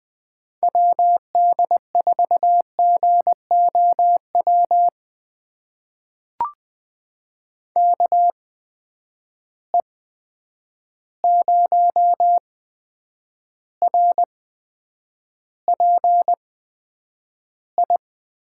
Morse code, heard as a keyed tone, about 21 wpm, spaced as for 9 wpm, 700 Hz. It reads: WD4GOW KE0RPI
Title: USA Call Signs - 6 Characters - 8x Character Spacing - 20wpm